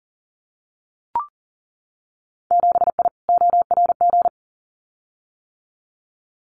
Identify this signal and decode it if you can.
Morse code, heard as a keyed tone, about 40 wpm, 700 Hz. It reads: 7I ORG
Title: Winter Field Day Exchanges Encore 40wpm